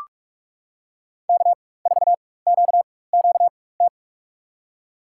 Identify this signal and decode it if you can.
Morse code, heard as a keyed tone, about 42 wpm, spaced as for 21 wpm, 700 Hz. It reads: K4QQT
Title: USA Call Signs - 5 Characters - 4x Character Spacing - 45wpm